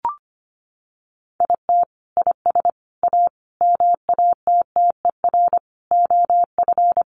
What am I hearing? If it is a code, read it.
Morse code, heard as a keyed tone, about 25 wpm, 700 Hz. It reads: IT IS A MATTER OF
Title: Sentences from Top 1400 Words 25wpm